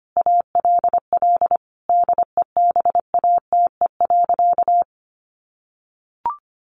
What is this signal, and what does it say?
Morse code, heard as a keyed tone, about 25 wpm, 700 Hz. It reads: ALL DEBATE.
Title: Sentences from Top 1100 Words 25wpm